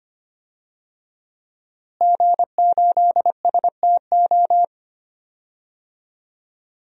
Morse code, heard as a keyed tone, about 25 wpm, 700 Hz. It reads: G8STO
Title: DXCC-100 Call Signs - 5 Characters - Repeated 3x - 25wpm